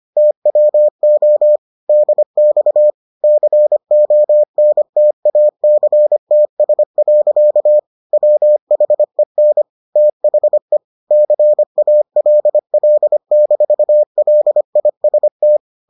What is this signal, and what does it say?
Morse code, heard as a keyed tone, about 25 wpm, 600 Hz. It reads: TWO DX CONTACTS. WHEN THE CALL-LIST